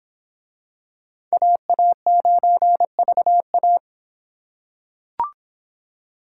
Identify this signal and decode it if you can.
Morse code, heard as a keyed tone, about 23 wpm, 700 Hz. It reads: AA9VA